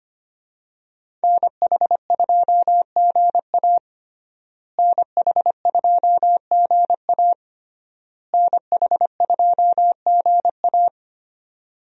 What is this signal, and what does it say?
Morse code, heard as a keyed tone, about 25 wpm, 700 Hz. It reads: NH2GA NH2GA NH2GA
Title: USA Call Signs - 5 Characters - Repeated 3x - 25wpm